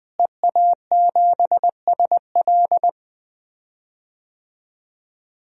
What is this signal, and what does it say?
Morse code, heard as a keyed tone, about 20 wpm, 700 Hz. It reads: EA7SL
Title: DXCC-100 Call Signs - 5 Characters - Repeated 3x - 20wpm